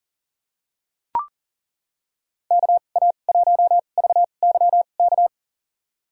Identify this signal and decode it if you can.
Morse code, heard as a keyed tone, about 39 wpm, spaced as for 29 wpm, 700 Hz. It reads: KA1VYK
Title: USA Call Signs - 2x Character Spacing - 40wpm